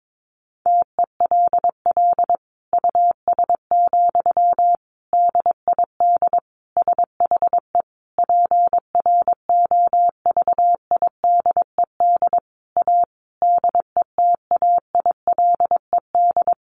Morse code, heard as a keyed tone, about 22 wpm, 700 Hz. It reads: TELL US, DID SHE PROVIDED A DETAILED
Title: Sentences from Top 1500 Words 22wpm